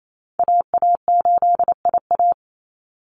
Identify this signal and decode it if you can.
Morse code, heard as a keyed tone, about 28 wpm, 700 Hz. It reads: AA8IA
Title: Call Signs Repeated 2x - 28wpm